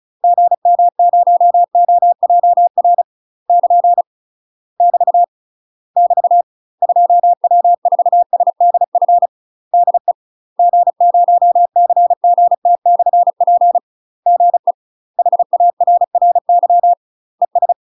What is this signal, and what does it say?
Morse code, heard as a keyed tone, about 34 wpm, 700 Hz. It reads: GM0OJR <KN> <BT> <BT> 2W4SDF DE G0CGT/P GE HARRY ES
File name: UK MASTER QSOs TRAINING / 35WPM / UK MASTER QSOs 0042